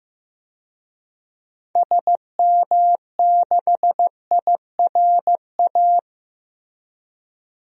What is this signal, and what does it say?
Morse code, heard as a keyed tone, about 15 wpm, 700 Hz. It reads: SM6IRA
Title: DXCC-100 Call Signs - 6 Characters - Repeated 3x - 15wpm